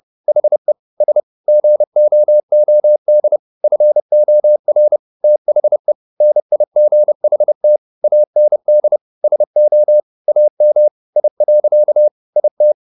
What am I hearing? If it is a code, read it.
Morse code, heard as a keyed tone, about 30 wpm, 600 Hz. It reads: HE S GOOD FOR THE NIGHT AND SO AM I. IT